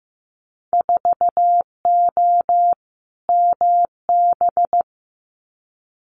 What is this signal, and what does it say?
Morse code, heard as a keyed tone, about 15 wpm, 700 Hz. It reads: 4O MB
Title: Winter Field Day Exchanges Encore 15wpm